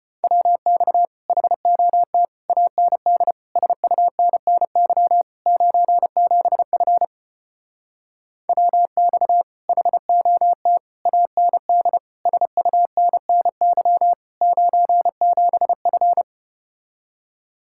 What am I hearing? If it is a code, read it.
Morse code, heard as a keyed tone, about 33 wpm, 700 Hz. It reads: WX HOT AND SUNNY 97F WX HOT AND SUNNY 97F